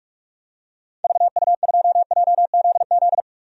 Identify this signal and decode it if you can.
Morse code, heard as a keyed tone, about 46 wpm, 700 Hz. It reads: VU2JZZ
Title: DXCC-100 Call Signs - 6 Characters - 45wpm